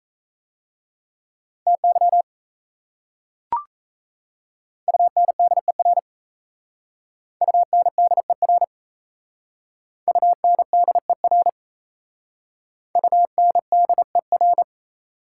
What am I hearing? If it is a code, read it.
Morse code, heard as a keyed tone, about 34 wpm, 700 Hz. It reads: TY UNDER UNDER UNDER UNDER
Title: Speed-Racing Prefixes and Suffixes 28wpm